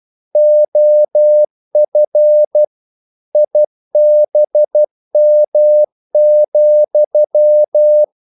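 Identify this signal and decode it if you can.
Morse code, heard as a keyed tone, about 12 wpm, 600 Hz. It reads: OF IBM,
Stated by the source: Morse practice sending English quotations.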